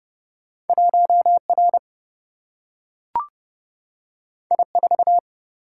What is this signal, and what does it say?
Morse code, heard as a keyed tone, about 30 wpm, 700 Hz. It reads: 1 R I 4
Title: Single-Letter Single-Number 30wpm